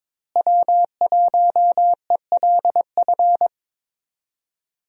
Morse code, heard as a keyed tone, about 22 wpm, 700 Hz. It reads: W1ELF
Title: USA Call Signs - 5 Characters - Repeated 2x - 22wpm